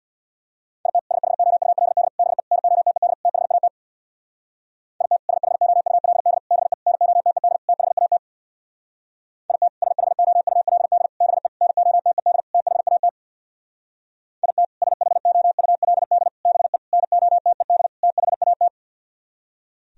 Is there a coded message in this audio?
Yes: Morse code from 0.9 to 18.7 s, heard as a keyed tone, about 64 wpm, 690 Hz. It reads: IT SHOULD BE NOTED THAT IT SHOULD BE NOTED THAT IT SHOULD BE NOTED THAT IT SHOULD BE NOTED THAT